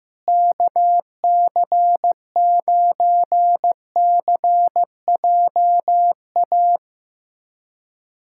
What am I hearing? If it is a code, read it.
Morse code, heard as a keyed tone, about 15 wpm, 700 Hz. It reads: KC9CJA
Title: USA Callsign DE Callsign - 6 Characters - 15wpm